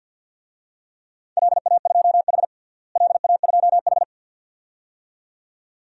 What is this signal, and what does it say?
Morse code, heard as a keyed tone, about 52 wpm, 700 Hz. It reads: LA2H LA2H
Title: DXCC-100 Call Signs - 4 Characters - Repeated 2x - 50wpm